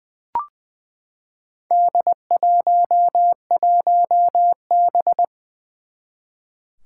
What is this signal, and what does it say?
Morse code, heard as a keyed tone, about 20 wpm, 700 Hz. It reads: D11B
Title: Four Challenging Letter-Number Combinations 20wpm